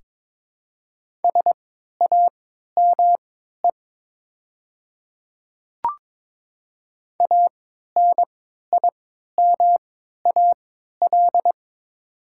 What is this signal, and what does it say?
Morse code, heard as a keyed tone, about 22 wpm, spaced as for 12 wpm, 700 Hz. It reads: SAME ANIMAL
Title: 3x Character Spacing - Top 200 Words - 22wpm